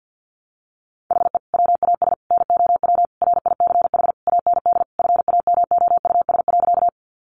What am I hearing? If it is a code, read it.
Morse code, heard as a keyed tone, about 50 wpm, 700 Hz. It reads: HE WAS NOW RICH AND FAMOUS.
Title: Sentences from Top 1500 Words 50wpm